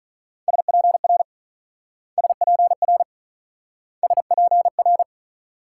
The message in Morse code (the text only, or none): SPR SPR SPR